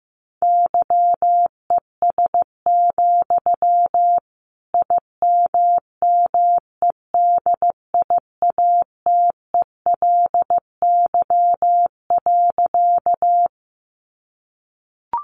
Morse code, heard as a keyed tone, about 15 wpm, 700 Hz. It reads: YES, IMMEDIATELY.
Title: Sentences from Top 1100 Words 15wpm